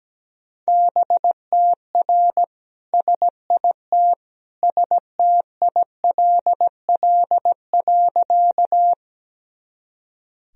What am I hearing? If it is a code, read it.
Morse code, heard as a keyed tone, about 17 wpm, 700 Hz. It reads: BTR SIT STILL.
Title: Sentences from Top 500 Words and Abbreviations 17wpm